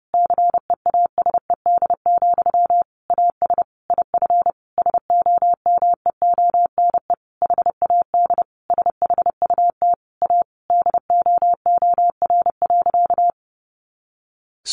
Morse code, heard as a keyed tone, about 30 wpm, 700 Hz. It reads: CEASED, AS IF SOMEONE HAD SHUT A DOOR.
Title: A Sound Of Thunder 30wpm